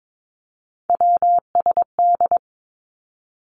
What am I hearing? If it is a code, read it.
Morse code, heard as a keyed tone, about 22 wpm, 700 Hz. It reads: WSD
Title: Three Challenging Letter-Number Combinations 22wpm